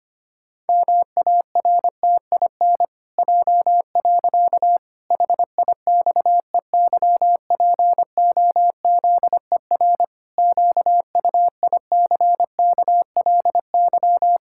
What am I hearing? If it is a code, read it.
Morse code, heard as a keyed tone, about 25 wpm, 700 Hz. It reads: MARTIN J. HIXEYPOZER QUICKLY
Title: Pangram Challenge v2 - 25wpm